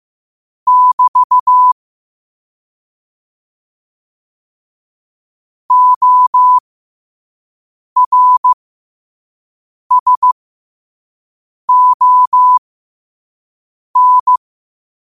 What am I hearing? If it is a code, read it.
Morse code, heard as a keyed tone, about 15 wpm, spaced as for 5 wpm, 1000 Hz. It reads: <BT> ORSON
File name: 5wpm